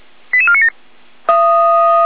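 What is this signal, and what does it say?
Five-tone selective call (ZVEI), heard as quick runs of five tones, tones of about 80 ms. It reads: 88497
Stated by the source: ZVEI five-tone selective call received by SDR.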